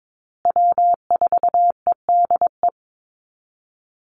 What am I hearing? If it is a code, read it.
Morse code, heard as a keyed tone, about 22 wpm, 700 Hz. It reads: W4EDE
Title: Call Signs Repeated 2x - 22wpm